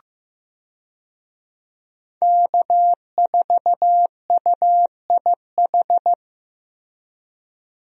Morse code, heard as a keyed tone, about 15 wpm, 700 Hz. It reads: K4UIH
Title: USA Call Signs - 5 Characters - Repeated 2x - 15wpm